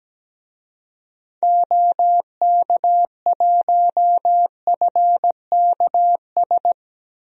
Morse code, heard as a keyed tone, about 17 wpm, 700 Hz. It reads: OK1FKS
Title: DXCC-100 Call Signs - 6 Characters - Repeated 2x - 17wpm